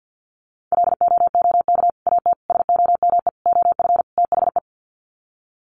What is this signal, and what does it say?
Morse code, heard as a keyed tone, about 50 wpm, 700 Hz. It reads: LOOK AT SOME OF THE
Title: Sets of 5 Words Encore 50wpm